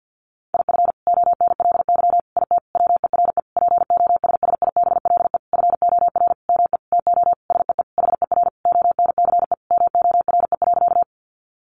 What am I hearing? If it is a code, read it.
Morse code, heard as a keyed tone, about 50 wpm, 700 Hz. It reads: IF ONLY IT WERE POSSIBLE FOR ME TO SEE HER ONCE MORE.
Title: Sentences from Top 500 Words 50wpm